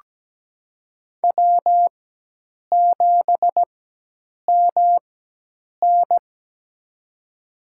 Morse code, heard as a keyed tone, about 17 wpm, spaced as for 8 wpm, 700 Hz. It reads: W7MN
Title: USA Call Signs - 4x Character Spacing - 17wpm